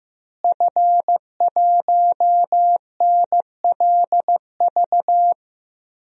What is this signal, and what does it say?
Morse code, heard as a keyed tone, about 15 wpm, 700 Hz. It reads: F1NLV
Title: DXCC-100 Call Signs - 5 Characters - Repeated 3x - 15wpm